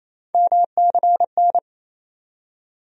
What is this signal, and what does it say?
Morse code, heard as a keyed tone, about 28 wpm, 700 Hz. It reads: MCN
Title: USA Call Sign Postfixes - 3 Letters 28wpm